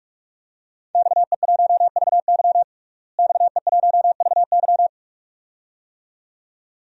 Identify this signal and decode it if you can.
Morse code, heard as a keyed tone, about 46 wpm, 700 Hz. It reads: XE1VY XE1VY
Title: DXCC-100 Call Signs - 5 Characters - Repeated 2x - 45wpm